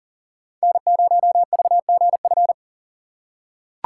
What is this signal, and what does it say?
Morse code, heard as a keyed tone, about 40 wpm, 700 Hz. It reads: N0VGF
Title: USA Call Signs - 5 Characters - 2x Character Spacing - 40wpm